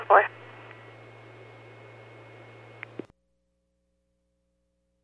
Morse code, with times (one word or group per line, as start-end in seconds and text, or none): none